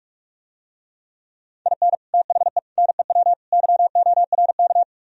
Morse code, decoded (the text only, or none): IN THE NEW YORK